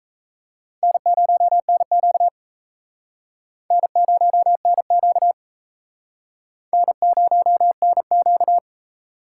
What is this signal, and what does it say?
Morse code, heard as a keyed tone, about 35 wpm, 700 Hz. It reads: N0NQ N0NQ N0NQ